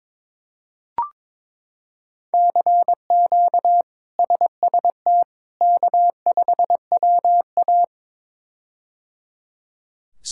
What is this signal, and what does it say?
Morse code, heard as a keyed tone, about 22 wpm, 700 Hz. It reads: CQ SST K5WA
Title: K1USN SST QSO Practice - 22wpm